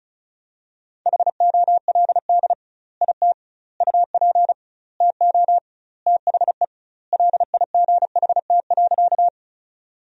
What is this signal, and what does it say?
Morse code, heard as a keyed tone, about 36 wpm, 700 Hz. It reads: HOLD IT UP TO THE LIGHT.
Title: 2x Word-Spacing - Sentences from Top 400 Words - 35wpm